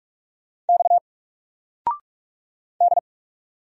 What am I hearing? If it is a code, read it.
Morse code, heard as a keyed tone, about 44 wpm, 700 Hz. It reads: X D
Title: Single Letter-Number 45wpm